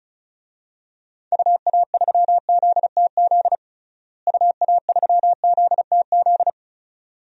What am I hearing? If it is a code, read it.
Morse code, heard as a keyed tone, about 34 wpm, 700 Hz. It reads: UA3ZTZ UA3ZTZ
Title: DXCC-100 Call Signs - 6 Characters - Repeated 2x - 35wpm